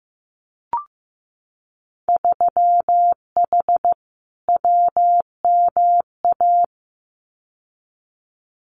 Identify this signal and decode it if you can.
Morse code, heard as a keyed tone, about 15 wpm, 700 Hz. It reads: 3H WMA
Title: Winter Field Day Exchanges Encore 15wpm